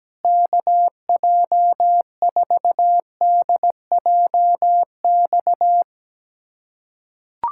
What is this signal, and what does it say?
Morse code, heard as a keyed tone, about 17 wpm, 700 Hz. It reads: KJ4DJX
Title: USA Callsign DE Callsign - 6 Characters - 17wpm